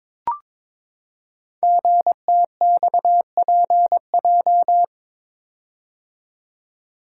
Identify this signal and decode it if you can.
Morse code, heard as a keyed tone, about 23 wpm, 700 Hz. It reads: GTXPJ
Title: Code Groups - 5 Letters - 22wpm